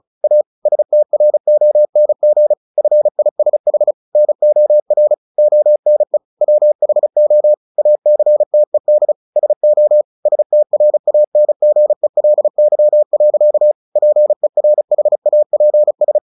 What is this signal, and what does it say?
Morse code, heard as a keyed tone, about 34 wpm, 600 Hz. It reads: A STRONG FISH NOR ONE WHO ACTED SO STRANGELY. PERHAPS
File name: the old man and the sea / the old man and the sea 1 35wpm